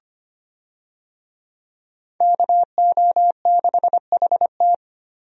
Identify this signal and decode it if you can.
Morse code, heard as a keyed tone, about 25 wpm, 700 Hz. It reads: KO6HT
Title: USA Call Signs - 5 Characters - Repeated 2x - 25wpm